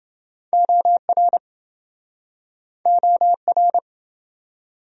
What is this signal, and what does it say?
Morse code, heard as a keyed tone, about 29 wpm, 700 Hz. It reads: OR OR